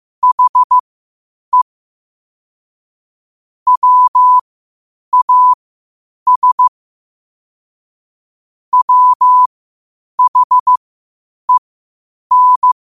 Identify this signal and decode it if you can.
Morse code, heard as a keyed tone, about 15 wpm, spaced as for 8 wpm, 1000 Hz. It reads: HE WAS WHEN
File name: qotdfiles / 8wpm